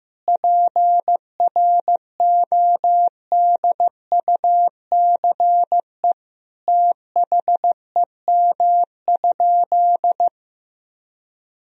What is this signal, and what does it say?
Morse code, heard as a keyed tone, about 15 wpm, 700 Hz. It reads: PRODUCE THEM?
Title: Sentences from Top 1000 Words - Review Part 3 - 15wpm